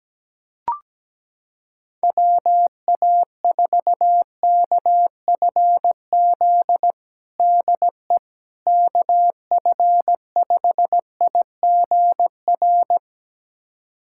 Morse code, heard as a keyed tone, about 17 wpm, 700 Hz. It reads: WA4KFZ DE KF5IGR
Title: USA Callsign DE Callsign - 6 Characters - 17wpm